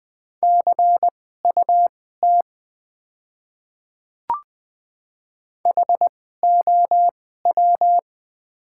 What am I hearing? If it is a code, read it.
Morse code, heard as a keyed tone, about 20 wpm, spaced as for 15 wpm, 700 Hz. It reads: CUT HOW
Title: 2x Character Spacing - Top 300 Words - 20wpm